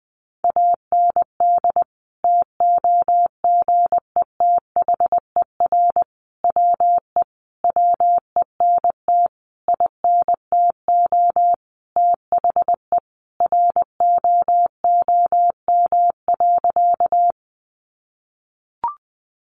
Morse code, heard as a keyed tone, about 20 wpm, 700 Hz. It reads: AND TOGETHER WE WENT INTO THE ROOM.
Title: Sentences from Top 600 Words 20wpm